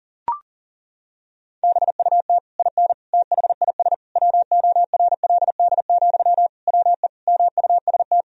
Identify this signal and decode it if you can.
Morse code, heard as a keyed tone, about 40 wpm, 700 Hz. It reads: BUT IN THIS WORLD, WE MUST
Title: Sentences from Top 1000 Words - Review Part 4 - 40wpm